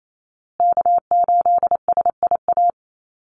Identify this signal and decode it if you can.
Morse code, heard as a keyed tone, about 28 wpm, 700 Hz. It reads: K8SIA
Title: Call Signs Repeated 2x - 28wpm